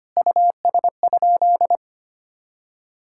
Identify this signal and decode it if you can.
Morse code, heard as a keyed tone, about 25 wpm, 700 Hz. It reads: US?
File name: practice / 25wpm / Sentences with 3-Word Phrases - 25wpm